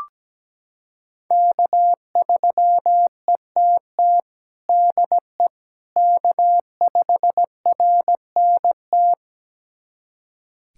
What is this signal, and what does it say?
Morse code, heard as a keyed tone, about 17 wpm, 700 Hz. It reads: K3ETT DE K5RNT
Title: USA Callsign DE Callsign - 5 Characters - 17wpm